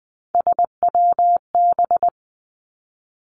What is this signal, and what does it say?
Morse code, heard as a keyed tone, about 20 wpm, 700 Hz. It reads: SWB
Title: Three Challenging Letter-Number Combinations 20wpm